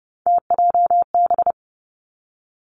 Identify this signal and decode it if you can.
Morse code, heard as a keyed tone, about 30 wpm, 700 Hz. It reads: TJB
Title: Three Challenging Letter-Number Combinations 30wpm